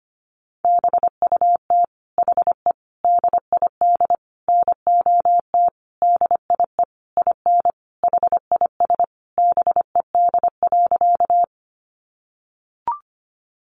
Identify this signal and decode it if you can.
Morse code, heard as a keyed tone, about 25 wpm, 700 Hz. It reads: BUT HE DID NOT DIE IN HIS BED.
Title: Sentences from Top 1300 Words 25wpm